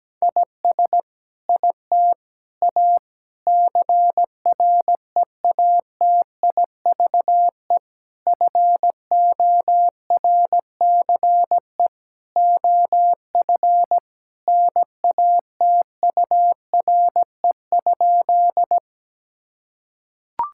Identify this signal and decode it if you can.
Morse code, heard as a keyed tone, about 17 wpm, 700 Hz. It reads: IS IT A CREATIVE FORCE OF NATURE?
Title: Sentences from Top 1000 Words - Review Part 5 - 17wpm